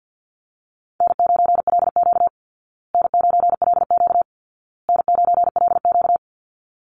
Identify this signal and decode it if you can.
Morse code, heard as a keyed tone, about 50 wpm, 700 Hz. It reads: N9LQ N9LQ N9LQ
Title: Easy Call Signs Repeated 3x - 50wpm